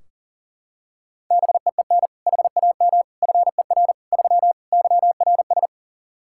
Morse code, heard as a keyed tone, about 40 wpm, 700 Hz. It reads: BEEN HAM FER 3 YRS
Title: K1USN SST QSO Practice 40wpm